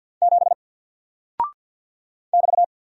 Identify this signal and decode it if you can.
Morse code, heard as a keyed tone, about 49 wpm, 700 Hz. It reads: 7 <BT>
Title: Single Character (Letter:Number:Punctuation:Prosign) 50wpm